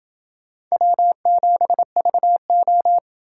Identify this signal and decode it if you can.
Morse code, heard as a keyed tone, about 28 wpm, 700 Hz. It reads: W7VO